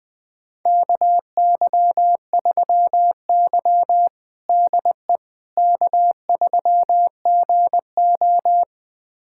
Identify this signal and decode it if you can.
Morse code, heard as a keyed tone, about 20 wpm, 700 Hz. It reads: KY3Y DE K3GO
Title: USA Callsign DE Callsign - 4 Characters - 20wpm